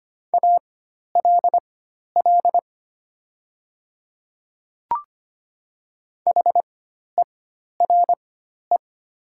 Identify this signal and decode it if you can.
Morse code, heard as a keyed tone, about 25 wpm, spaced as for 12 wpm, 700 Hz. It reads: ALL HERE
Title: 4x Character Spacing - Top 200 Words - 25wpm